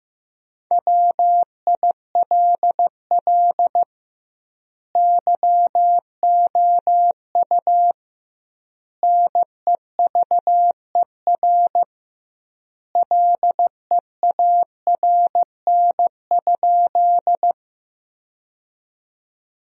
Morse code, heard as a keyed tone, about 15 wpm, 700 Hz. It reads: WILL YOU NEVER LEARN?
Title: 2x Word-Spacing - Sentences from Top 300 Words - 15wpm